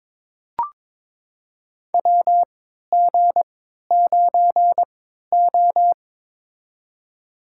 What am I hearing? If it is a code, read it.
Morse code, heard as a keyed tone, about 22 wpm, spaced as for 13 wpm, 700 Hz. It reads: WG9O
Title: USA Call Signs - 3x Character Spacing - 22wpm